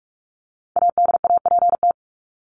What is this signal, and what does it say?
Morse code, heard as a keyed tone, about 44 wpm, 700 Hz. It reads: ADAPT
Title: Top 1501-1750 Words 45wpm